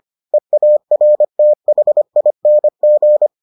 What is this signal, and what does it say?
Morse code, heard as a keyed tone, about 25 wpm, 600 Hz. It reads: EARTHING